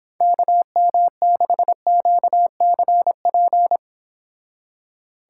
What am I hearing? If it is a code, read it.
Morse code, heard as a keyed tone, about 26 wpm, 700 Hz. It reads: KM6QCP